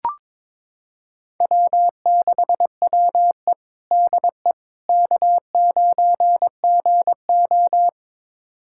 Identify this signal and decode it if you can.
Morse code, heard as a keyed tone, about 22 wpm, 700 Hz. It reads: W6WE DE K9GO
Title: USA Callsign DE Callsign - 4 Characters - 22wpm